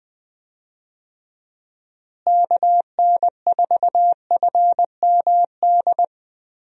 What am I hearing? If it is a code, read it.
Morse code, heard as a keyed tone, about 20 wpm, 700 Hz. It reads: KN4FMD
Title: USA Call Signs - 6 Characters - Repeated 3x - 20wpm